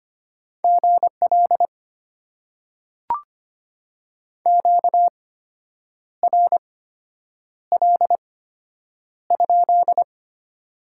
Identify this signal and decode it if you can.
Morse code, heard as a keyed tone, about 25 wpm, spaced as for 8 wpm, 700 Hz. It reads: GL QRL?
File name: Top 90 QSO Elements - 8x Character Spacing - 25wpm